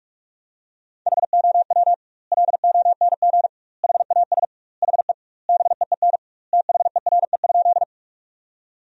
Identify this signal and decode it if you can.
Morse code, heard as a keyed tone, about 41 wpm, 700 Hz. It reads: HOW LONG HAS HE BEEN THERE?
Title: 2x Word-Spacing - Sentences from Top 100 Words - 45wpm